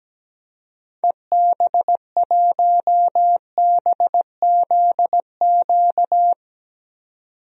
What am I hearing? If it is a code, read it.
Morse code, heard as a keyed tone, about 17 wpm, 700 Hz. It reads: EB1BZQ
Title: DXCC-100 Call Signs - 6 Characters - 17wpm